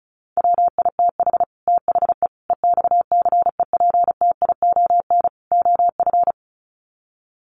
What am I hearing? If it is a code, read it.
Morse code, heard as a keyed tone, about 34 wpm, 700 Hz. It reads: WITH THE EXCEPTION OF
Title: Sets of 4 Words 35wpm